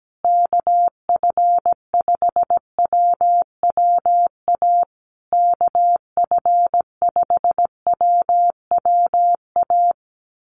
Morse code, heard as a keyed tone, about 17 wpm, 700 Hz. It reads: KF5WWA KF5WWA
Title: USA Call Signs Repeated 3x 17wpm